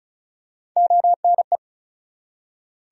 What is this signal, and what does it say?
Morse code, heard as a keyed tone, about 36 wpm, 700 Hz. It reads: ONE
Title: Top 100 Words 35wpm